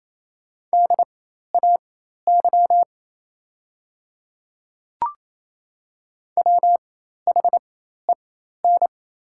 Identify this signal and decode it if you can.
Morse code, heard as a keyed tone, about 28 wpm, spaced as for 13 wpm, 700 Hz. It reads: DAY WHEN
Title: 4x Character Spacing - Top 100 Words - 28wpm